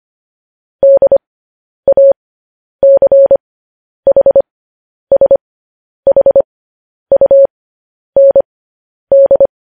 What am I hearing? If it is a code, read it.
Morse code, heard as a keyed tone, about 25 wpm, spaced as for 10 wpm, 550 Hz. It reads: DACHSHUND
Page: news headlines in morse code at 10 wpm